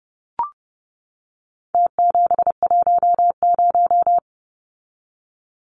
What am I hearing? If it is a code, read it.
Morse code, heard as a keyed tone, about 30 wpm, 700 Hz. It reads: T710
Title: Numbers - Contest Style - 4-Digit 30wpm